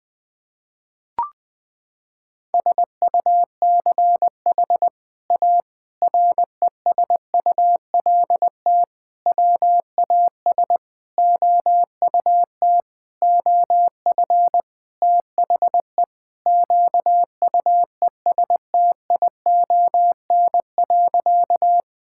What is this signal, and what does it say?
Morse code, heard as a keyed tone, about 20 wpm, 700 Hz. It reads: SUCH A RESULT WAS OUT OF THE QUESTION.
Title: Sentences from Top 1000 Words - Review Part 5 - 20wpm